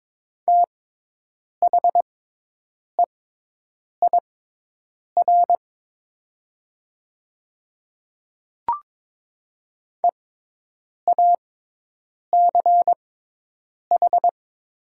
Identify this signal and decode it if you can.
Morse code, heard as a keyed tone, about 22 wpm, spaced as for 8 wpm, 700 Hz. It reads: THEIR EACH
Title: 6x Character Spacing - Top 100 Words - 22wpm